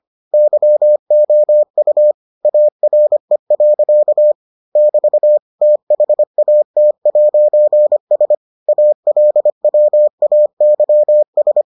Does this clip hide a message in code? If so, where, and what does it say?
Yes: Morse code from 0.3 to 11.6 s, heard as a keyed tone, about 25 wpm, 600 Hz. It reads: YOU ARE. <BT> THAT'S ALWAYS